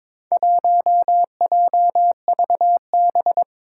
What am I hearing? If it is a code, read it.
Morse code, heard as a keyed tone, about 22 wpm, 700 Hz. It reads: 1JVB
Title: Four Challenging Letter-Number Combinations 22wpm